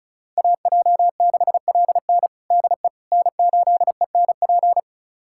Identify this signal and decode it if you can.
Morse code, heard as a keyed tone, about 34 wpm, 700 Hz. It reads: AJ6LN DE N8ENP
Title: USA Callsign DE Callsign - 5 Characters - 35wpm